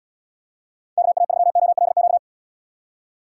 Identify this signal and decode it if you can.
Morse code, heard as a keyed tone, about 77 wpm, 700 Hz. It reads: KA3OLQ